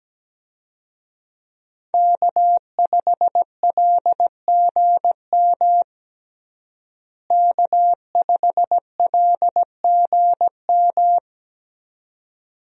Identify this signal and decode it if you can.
Morse code, heard as a keyed tone, about 17 wpm, 700 Hz. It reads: K5LGM K5LGM
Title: USA Call Signs - 5 Characters - Repeated 2x - 17wpm